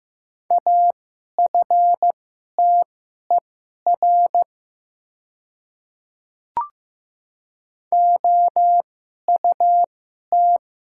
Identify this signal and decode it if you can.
Morse code, heard as a keyed tone, about 15 wpm, spaced as for 11 wpm, 700 Hz. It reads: AFTER OUT
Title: 2x Character Spacing - Top 500 Words - 15wpm